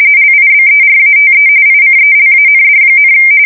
RTTY, about 45.45 baud. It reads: LIBERATE ICELAND,
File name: fsk170 / battleshipnj